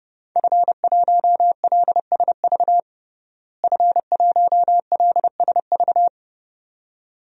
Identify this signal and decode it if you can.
Morse code, heard as a keyed tone, about 30 wpm, 700 Hz. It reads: F1LSV F1LSV
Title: DXCC-100 Call Signs - 5 Characters - Repeated 2x - 30wpm